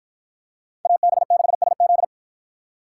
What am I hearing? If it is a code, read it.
Morse code, heard as a keyed tone, about 51 wpm, 700 Hz. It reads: AB6SZ